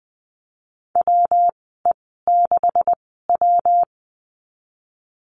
Morse code, heard as a keyed tone, about 20 wpm, spaced as for 14 wpm, 700 Hz. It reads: WE6W
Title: 161 Lesson 26g - Letter g - Call Signs 20wpm